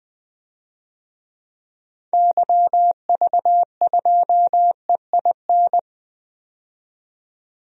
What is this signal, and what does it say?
Morse code, heard as a keyed tone, about 20 wpm, 700 Hz. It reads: YV2EIN
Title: DXCC-100 Call Signs - 6 Characters - 20wpm